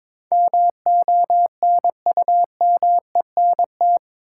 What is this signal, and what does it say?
Morse code, heard as a keyed tone, about 22 wpm, 700 Hz. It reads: MONUMENT
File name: practice / 22wpm / Eight Letter Words - 22wpm